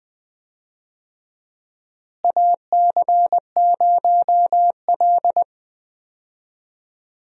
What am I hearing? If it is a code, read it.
Morse code, heard as a keyed tone, about 20 wpm, 700 Hz. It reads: AC0L
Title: USA Call Signs - 4 Characters - Repeated 3x - 20wpm